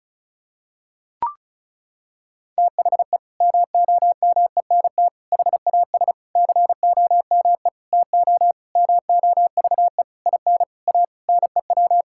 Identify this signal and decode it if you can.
Morse code, heard as a keyed tone, about 36 wpm, 700 Hz. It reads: THE MOMENT HAS COME TO MOVE IN A NEW
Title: Sentences from Top 1000 Words - Review Part 5 - 35wpm